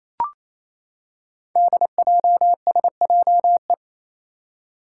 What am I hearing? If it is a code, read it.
Morse code, heard as a keyed tone, about 28 wpm, 700 Hz. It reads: DJSJE
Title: Five Challenging Letter-Number Combinations 28wpm